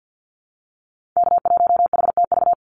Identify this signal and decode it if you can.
Morse code, heard as a keyed tone, about 50 wpm, 700 Hz. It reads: K1HTV
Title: Call Signs - 50wpm